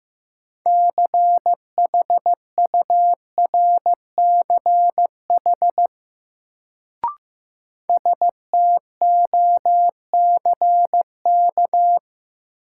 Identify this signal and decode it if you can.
Morse code, heard as a keyed tone, about 15 wpm, 700 Hz. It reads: CHURCH STOCK
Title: Sets of 5 Words - New Word Drill - 15wpm